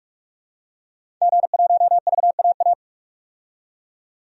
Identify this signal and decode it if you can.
Morse code, heard as a keyed tone, about 47 wpm, 700 Hz. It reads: G1VAA
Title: DXCC-100 Call Signs - 5 Characters - 45wpm